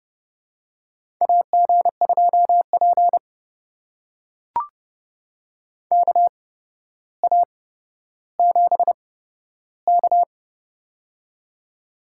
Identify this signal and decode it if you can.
Morse code, heard as a keyed tone, about 32 wpm, spaced as for 13 wpm, 700 Hz. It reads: AG2P KA7K
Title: USA Call Signs - 4 Characters - 8x Character Spacing - 30wpm